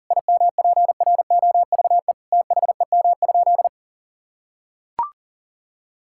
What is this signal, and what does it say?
Morse code, heard as a keyed tone, about 37 wpm, 700 Hz. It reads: IMPROVE THEM?
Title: Sentences from Top 1000 Words - Review Part 5 - 40wpm